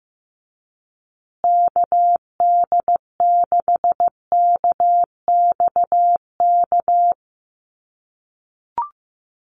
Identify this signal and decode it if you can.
Morse code, heard as a keyed tone, about 15 wpm, 700 Hz. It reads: KD6KXK
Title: USA Call Signs Repeated 2x 15wpm